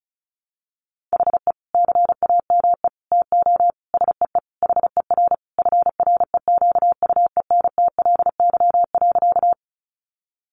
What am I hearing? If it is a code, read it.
Morse code, heard as a keyed tone, about 34 wpm, 700 Hz. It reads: HE CAME TO SEE HER FREQUENTLY.
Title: Sentences from Top 1300 Words 35wpm